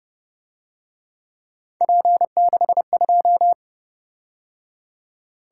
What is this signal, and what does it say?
Morse code, heard as a keyed tone, about 29 wpm, 700 Hz. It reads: P62